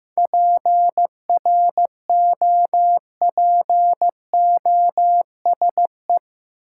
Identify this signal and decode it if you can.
Morse code, heard as a keyed tone, about 16 wpm, 700 Hz. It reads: PROPOSE
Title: Seven Letter Words - 15wpm